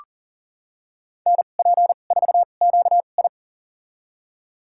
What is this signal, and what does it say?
Morse code, heard as a keyed tone, about 39 wpm, spaced as for 29 wpm, 700 Hz. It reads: NP4QI
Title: USA Call Signs - 2x Character Spacing - 40wpm